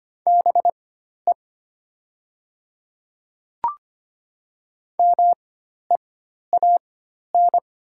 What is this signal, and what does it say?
Morse code, heard as a keyed tone, about 25 wpm, spaced as for 12 wpm, 700 Hz. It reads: BE MEAN
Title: 4x Character Spacing - Top 200 Words - 25wpm